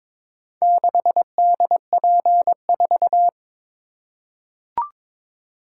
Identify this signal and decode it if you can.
Morse code, heard as a keyed tone, about 22 wpm, 700 Hz. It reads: 6DP4
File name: practice / 22wpm / Four Letter-Number Combinations - 22wpm